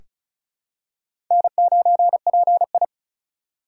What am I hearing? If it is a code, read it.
Morse code, heard as a keyed tone, about 36 wpm, 700 Hz. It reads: N9PI
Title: USA Call Signs - 4x Character Spacing - 35wpm